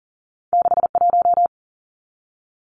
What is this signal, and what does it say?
Morse code, heard as a keyed tone, about 40 wpm, 700 Hz. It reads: 61
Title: Numbers - 2-Digit 40wpm